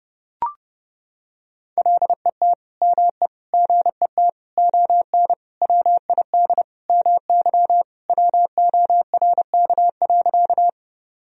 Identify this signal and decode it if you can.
Morse code, heard as a keyed tone, about 31 wpm, 700 Hz. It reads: LET ME GET ON WID MY WORK.
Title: Sentences from Top 300 Words and Abbreviations 30wpm